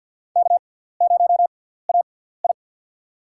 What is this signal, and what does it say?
Morse code, heard as a keyed tone, about 52 wpm, spaced as for 17 wpm, 700 Hz. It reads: K0AI
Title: USA Call Signs - 4 Characters - 6x Character Spacing - 50wpm